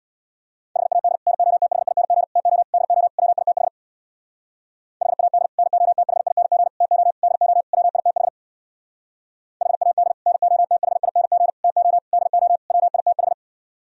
Morse code, heard as a keyed tone, about 64 wpm, 700 Hz. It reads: HAD NOTHING TO DO WITH HAD NOTHING TO DO WITH HAD NOTHING TO DO WITH